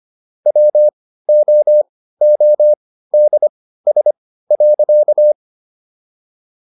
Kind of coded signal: Morse code

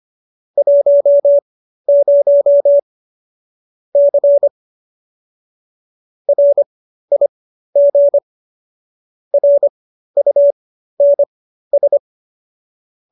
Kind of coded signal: Morse code

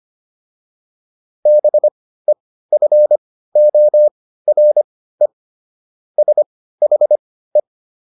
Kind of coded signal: Morse code

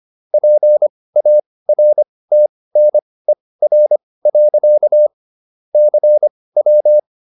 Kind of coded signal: Morse code

